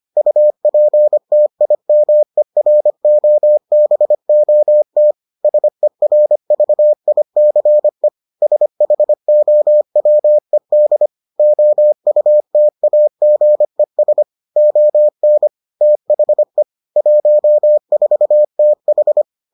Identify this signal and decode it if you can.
Morse code, heard as a keyed tone, about 25 wpm, 600 Hz. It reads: UPTIMEROBOT SERVICE SHOWED OUTAGES ON THE 14TH